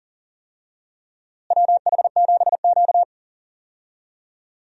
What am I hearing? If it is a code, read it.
Morse code, heard as a keyed tone, about 40 wpm, 700 Hz. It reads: WH7Q
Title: USA Call Signs - 4 Characters - Repeated 3x - 40wpm